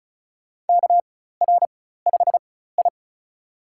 Morse code, heard as a keyed tone, about 36 wpm, spaced as for 16 wpm, 700 Hz. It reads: KR5I
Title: USA Call Signs - 4 Characters - 4x Character Spacing - 35wpm